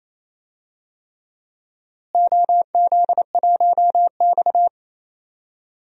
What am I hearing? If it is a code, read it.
Morse code, heard as a keyed tone, about 28 wpm, 700 Hz. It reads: OZ1X
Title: DXCC-100 Call Signs - 4 Characters - Repeated 3x - 28wpm